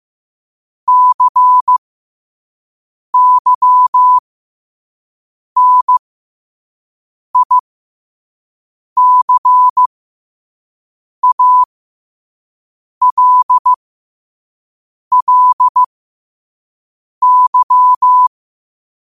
Morse code, heard as a keyed tone, about 15 wpm, spaced as for 5 wpm, 1000 Hz. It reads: CYNICALLY